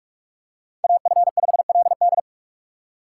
Morse code, heard as a keyed tone, about 46 wpm, 700 Hz. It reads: AF5LD
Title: USA Call Signs - 3x Character Spacing - 45wpm